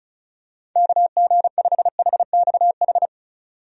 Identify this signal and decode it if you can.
Morse code, heard as a keyed tone, about 36 wpm, 700 Hz. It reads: KG5HXH
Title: USA Call Signs - 6 Characters - 35wpm